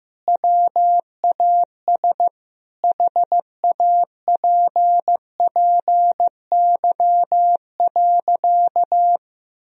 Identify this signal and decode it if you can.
Morse code, heard as a keyed tone, about 15 wpm, 700 Hz. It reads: WAS HAPPY.
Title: Sentences from Top 1000 Words - Review Part 1 - 15wpm